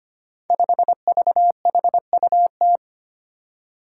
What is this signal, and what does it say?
Morse code, heard as a keyed tone, about 25 wpm, 700 Hz. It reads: 5VHUT
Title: Five Challenging Letter-Number Combinations 25wpm